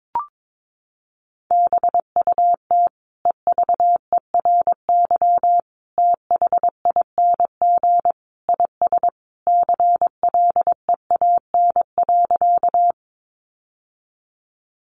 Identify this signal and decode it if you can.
Morse code, heard as a keyed tone, about 22 wpm, 700 Hz. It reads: BUT EVERYTHING IS CLEAN.
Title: Sentences from Top 1100 Words 22wpm